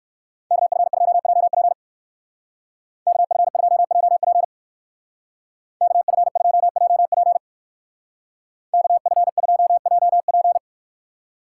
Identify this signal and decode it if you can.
Morse code, heard as a keyed tone, about 56 wpm, 700 Hz. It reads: KF2JP KF2JP KF2JP KF2JP
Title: Speed Racing - USA 2x2 Call Signs 45wpm